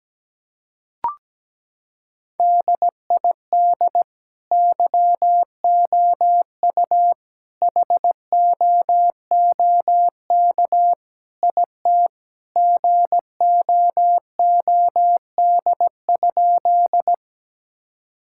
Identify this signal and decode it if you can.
Morse code, heard as a keyed tone, about 17 wpm, 700 Hz. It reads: DID YOU HOOK IT GOOD?
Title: Sentences from Top 1600 Words 17wpm